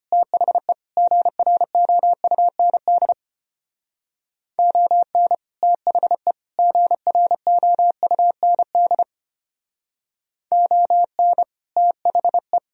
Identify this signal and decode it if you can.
Morse code, heard as a keyed tone, about 32 wpm, 700 Hz. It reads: THE GROUND ON THE GROUND ON THE